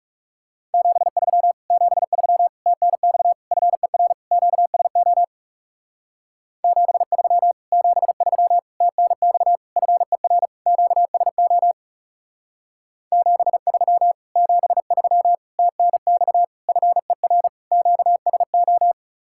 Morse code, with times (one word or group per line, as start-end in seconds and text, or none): 0.7-1.5 73
1.7-2.5 73
2.7-3.3 TNX
3.5-4.1 FER
4.3-5.2 QSO
6.6-7.5 73
7.7-8.6 73
8.8-9.6 TNX
9.8-10.5 FER
10.7-11.7 QSO
13.1-14.1 73
14.4-15.3 73
15.6-16.4 TNX
16.7-17.5 FER
17.7-18.9 QSO